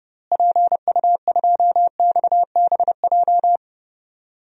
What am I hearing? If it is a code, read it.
Morse code, heard as a keyed tone, about 31 wpm, 700 Hz. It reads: PU2XBJ
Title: DXCC-100 Call Signs - 6 Characters - Repeated 2x - 30wpm